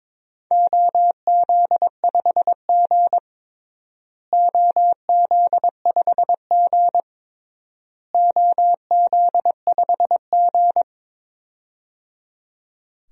Morse code, heard as a keyed tone, about 22 wpm, 700 Hz. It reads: OZ5G OZ5G OZ5G
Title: DXCC-100 Call Signs - 4 Characters - Repeated 3x - 22wpm